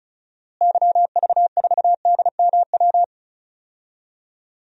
Morse code, heard as a keyed tone, about 34 wpm, 700 Hz. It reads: YV4DMW
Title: DXCC-100 Call Signs - 6 Characters - Repeated 2x - 35wpm